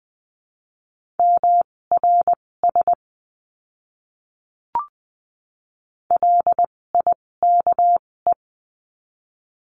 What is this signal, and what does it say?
Morse code, heard as a keyed tone, about 20 wpm, spaced as for 16 wpm, 700 Hz. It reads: MRS LIKE
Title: 256 Lesson 42a - Words 12-20wpm